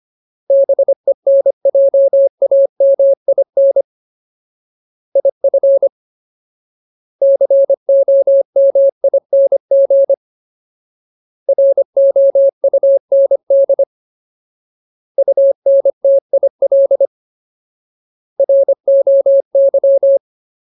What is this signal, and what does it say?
Morse code, heard as a keyed tone, about 25 wpm, 550 Hz. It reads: BENJAMIN IF COMING ROUND UNTIL ROY
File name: Ultimate Word List / 25WPM Ultimate